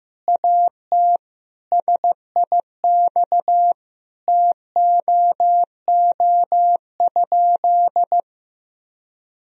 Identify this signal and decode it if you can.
Morse code, heard as a keyed tone, about 15 wpm, 700 Hz. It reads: AT SIX TOO?
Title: Sentences from Top 1000 Words - Review Part 2 - 15wpm